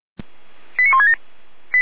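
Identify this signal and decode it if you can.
Five-tone selective call (ZVEI), heard as quick runs of five tones, tones of about 70 ms. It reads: 89167
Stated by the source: Received on an SDR as ZVEI five-tone call, address 89084.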